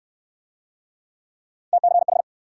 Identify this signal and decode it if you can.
Morse code, heard as a keyed tone, about 67 wpm, 700 Hz. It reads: TCH